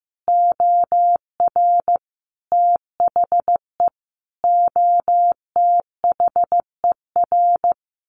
Morse code, heard as a keyed tone, about 15 wpm, 700 Hz. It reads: OR THE OTHER
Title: Sets of 5 Words 15wpm